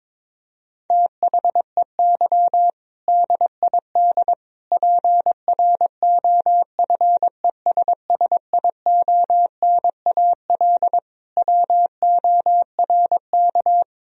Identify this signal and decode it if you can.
Morse code, heard as a keyed tone, about 22 wpm, 700 Hz. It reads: THEY DID PROFESSIONAL WORK
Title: Sentences from Top 1000 Words - Review Part 4 - 22wpm